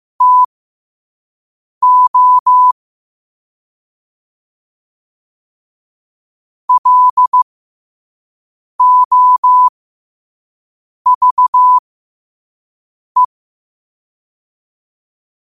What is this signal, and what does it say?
Morse code, heard as a keyed tone, about 15 wpm, spaced as for 5 wpm, 1000 Hz. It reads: TO LOVE
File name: qotdfiles / 5wpm